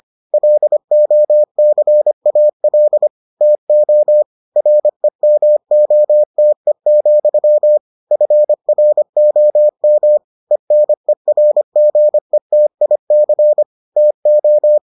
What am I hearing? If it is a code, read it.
Morse code, heard as a keyed tone, about 25 wpm, 600 Hz. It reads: LOCAL TO REMOTE, FROM ENERGETIC TO